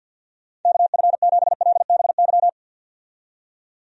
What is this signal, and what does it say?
Morse code, heard as a keyed tone, about 50 wpm, 700 Hz. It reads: KF7LBY
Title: USA Call Signs - 6 Characters - 50wpm